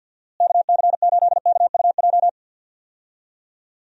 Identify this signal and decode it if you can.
Morse code, heard as a keyed tone, about 47 wpm, 700 Hz. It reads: KC8KUJ
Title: USA Call Signs - 6 Characters - Repeated 3x - 50wpm